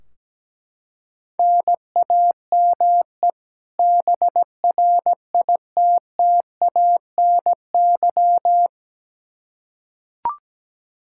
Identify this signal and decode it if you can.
Morse code, heard as a keyed tone, about 17 wpm, 700 Hz. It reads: NAME BRITTANY
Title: K1USN SST QSO Practice 17wpm